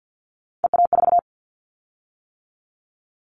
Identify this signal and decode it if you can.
Morse code, heard as a keyed tone, about 50 wpm, 700 Hz. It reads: EA4
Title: Callsign Prefixes for Top 25 DXCC - 50wpm